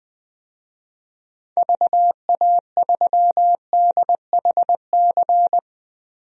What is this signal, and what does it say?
Morse code, heard as a keyed tone, about 20 wpm, 700 Hz. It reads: VA3DHC
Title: DXCC-100 Call Signs - 6 Characters - Repeated 2x - 20wpm